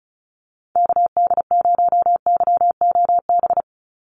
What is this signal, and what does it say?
Morse code, heard as a keyed tone, about 34 wpm, 700 Hz. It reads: KD0YOB
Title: NA SOTA Activator Callsigns 35wpm